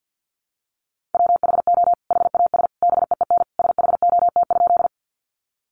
Morse code, heard as a keyed tone, about 50 wpm, 700 Hz. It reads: WHO HAS BEEN SHOT?
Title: Sentences from Top 1100 Words 50wpm